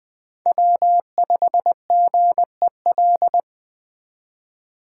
Morse code, heard as a keyed tone, about 21 wpm, 700 Hz. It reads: W5GEL
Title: Speed Racing - USA 1x3 Call Signs 17wpm